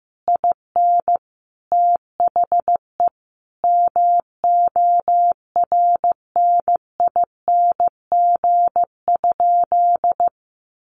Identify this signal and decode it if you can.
Morse code, heard as a keyed tone, about 15 wpm, 700 Hz. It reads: IN THE MORNING?
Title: Sentences from Top 1300 Words 15wpm